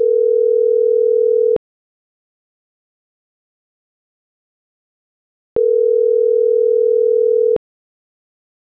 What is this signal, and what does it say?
Ringback tone (North America), 440 + 480 Hz.